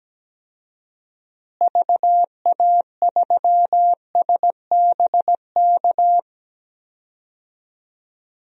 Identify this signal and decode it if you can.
Morse code, heard as a keyed tone, about 17 wpm, 700 Hz. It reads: VA3SBK
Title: DXCC-100 Call Signs - 6 Characters - Repeated 2x - 17wpm